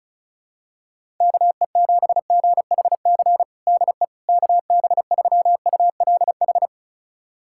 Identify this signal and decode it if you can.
Morse code, heard as a keyed tone, about 34 wpm, 700 Hz. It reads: KE7GHC DE KB3ULH
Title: USA Callsign DE Callsign - 6 Characters - 35wpm